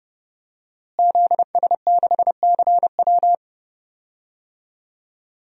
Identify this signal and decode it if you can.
Morse code, heard as a keyed tone, about 31 wpm, 700 Hz. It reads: ZS6CW
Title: DXCC-100 Call Signs - 5 Characters - Repeated 2x - 30wpm